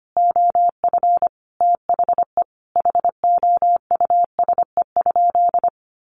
Morse code, heard as a keyed tone, about 25 wpm, 700 Hz. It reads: OF THE HOUSE?
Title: Sentences from Top 200 Words 25wpm